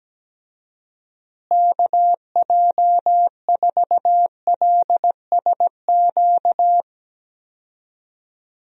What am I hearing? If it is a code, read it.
Morse code, heard as a keyed tone, about 17 wpm, 700 Hz. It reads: KJ4LSQ
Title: USA Call Signs - 6 Characters - Repeated 2x - 17wpm